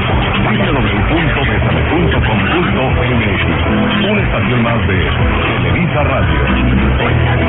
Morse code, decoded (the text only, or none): none